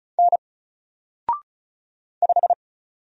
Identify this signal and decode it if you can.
Morse code, heard as a keyed tone, about 34 wpm, 700 Hz. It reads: N 5
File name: Single-Letter-Number-35wpm